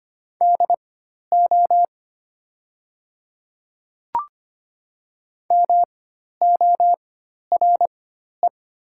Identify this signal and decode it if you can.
Morse code, heard as a keyed tone, about 25 wpm, spaced as for 12 wpm, 700 Hz. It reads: DO MORE
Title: 4x Character Spacing - Top 100 Words - 25wpm